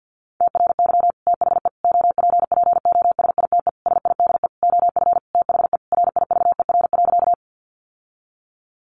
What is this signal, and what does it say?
Morse code, heard as a keyed tone, about 50 wpm, 700 Hz. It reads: TRY THE OPPOSITE SIDE OF THE RIVER.
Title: Sentences from Top 1300 Words 50wpm